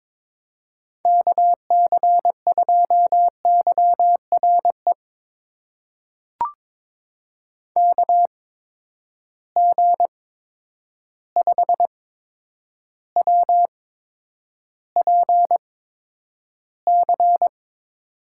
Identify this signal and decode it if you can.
Morse code, heard as a keyed tone, about 22 wpm, spaced as for 9 wpm, 700 Hz. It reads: KC2YRE KG5WPC
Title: USA Call Signs - 6 Characters - 8x Character Spacing - 22wpm